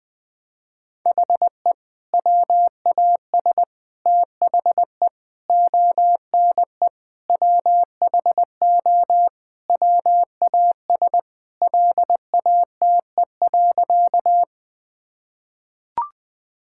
Morse code, heard as a keyed tone, about 20 wpm, 700 Hz. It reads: HE WAS THE ONE WHO WAS LATE.
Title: Sentences with 2-Word Phrases - 20wpm